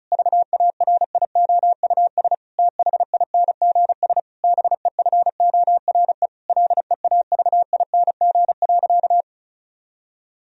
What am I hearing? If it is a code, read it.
Morse code, heard as a keyed tone, about 34 wpm, 700 Hz. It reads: VARIOUS THINGS BEFORE LEAVING.
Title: Sentences from Top 1000 Words - Review Part 4 - 35wpm